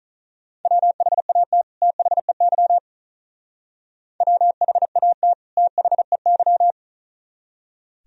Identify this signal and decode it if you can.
Morse code, heard as a keyed tone, about 44 wpm, 700 Hz. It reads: WHAT THEY WHAT THEY
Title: Speed-Racing - Sets of 2 Words 35wpm